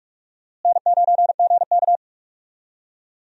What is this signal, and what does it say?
Morse code, heard as a keyed tone, about 43 wpm, 700 Hz. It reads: N9GK
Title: USA Call Signs - 4 Characters - Repeated 3x - 45wpm